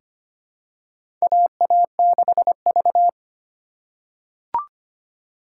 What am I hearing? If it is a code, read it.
Morse code, heard as a keyed tone, about 25 wpm, 700 Hz. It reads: AA6V
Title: USA Call Signs - 4 Characters - 8x Character Spacing - 25wpm